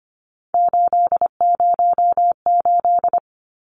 Morse code, heard as a keyed tone, about 25 wpm, 700 Hz. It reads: 808
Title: Numbers - 3-Digit 25wpm